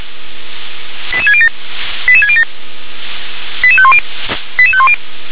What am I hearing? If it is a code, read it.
Five-tone selective call (ZVEI), heard as quick runs of five tones, tones of about 80 ms. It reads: 88607 88607 88510 88510
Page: /357/SDR/ZVEI/88742/2021.07.25/